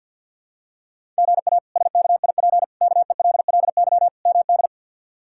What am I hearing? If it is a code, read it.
Morse code, heard as a keyed tone, about 52 wpm, 690 Hz. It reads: GA SKIP KELLY MD
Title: K1USN SST QSO Practice - 50wpm